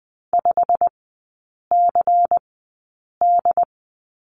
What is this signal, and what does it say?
Morse code, heard as a keyed tone, about 20 wpm, spaced as for 8 wpm, 700 Hz. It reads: 5CD
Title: 152 Lesson 25e - Letter p - Three Characters 20wpm